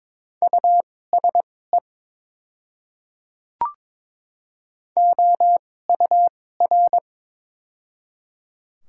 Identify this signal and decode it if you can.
Morse code, heard as a keyed tone, about 22 wpm, spaced as for 16 wpm, 700 Hz. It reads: USE OUR
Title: 2x Character Spacing - Top 300 Words - 22wpm